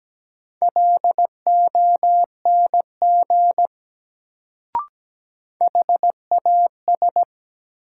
Morse code, heard as a keyed tone, about 17 wpm, 700 Hz. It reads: LONG HAS
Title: Sentences from Top 100 Words - New Word Drill - 17wpm